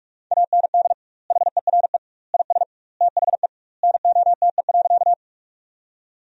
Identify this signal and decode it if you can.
Morse code, heard as a keyed tone, about 46 wpm, 700 Hz. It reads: AND HERE IS THE NOTE.
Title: 2x Word-Spacing - Sentences from Top 500 Words - 45wpm